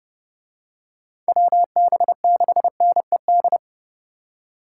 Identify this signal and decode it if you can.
Morse code, heard as a keyed tone, about 29 wpm, 700 Hz. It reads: WB6NED